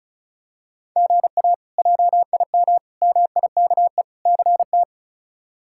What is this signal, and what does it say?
Morse code, heard as a keyed tone, about 34 wpm, 700 Hz. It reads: GA JIM MIKE CT
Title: K1USN SST QSO Practice - 35wpm